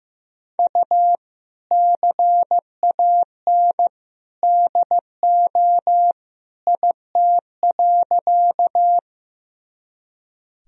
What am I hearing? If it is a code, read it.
Morse code, heard as a keyed tone, about 15 wpm, 700 Hz. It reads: U CAN DO IT.
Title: Sentences from Top 100 Words and Abbreviations 15wpm